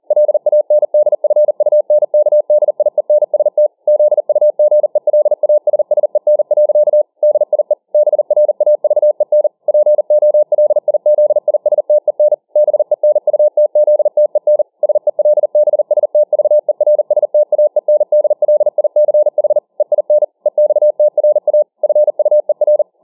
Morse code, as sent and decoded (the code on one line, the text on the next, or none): .-.. .- -. -.. ..-. ..- -. -.- -.. .. -. ... - / --.. ..- --. .-.. .- ... ... -. .-.-.- / -.. .. / -... .-. .- ...- . -. / .--. --- .-.. .. --.. .. ... - . -. / -... -. ..- - --.. - . -. / ... .-.. -... ... - ...- . .-. ... - .- . -. -.. .-.. .. -.-. .... / . .. -. / . -..- - .-. .- / ..-. ..- . .-.
LANDFUNKDIENST ZUGELASSEN. DIE BRAVEN POLIZISTEN BENUTZTEN SELBSTVERSTAENDLICH EIN EXTRA FUER